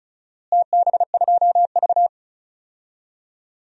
Morse code, heard as a keyed tone, about 34 wpm, 700 Hz. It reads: TB2V
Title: DXCC-100 Call Signs - 4 Characters - Repeated 3x - 35wpm